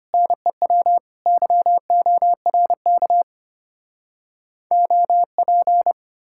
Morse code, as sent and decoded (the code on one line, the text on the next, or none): -. .-- / -.-- --- .-. -.- / --- .--.
NEW YORK OP